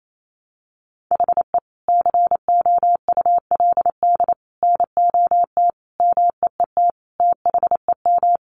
Morse code, heard as a keyed tone, about 28 wpm, 700 Hz. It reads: HE COULD NOT MEET THEM
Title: Sentences from Top 1100 Words 28wpm